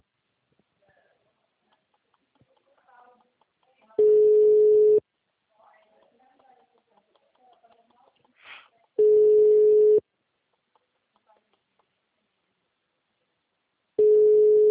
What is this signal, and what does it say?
Ringback tone (Europe, 425 Hz), with speech.
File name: out- -2013- - - .88823